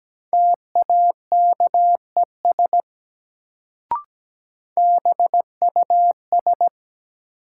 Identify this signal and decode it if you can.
Morse code, heard as a keyed tone, about 17 wpm, 700 Hz. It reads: TAKES BUS
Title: Sentences from Top 1000 Words - Review Part 6 - New Word Drill - 17wpm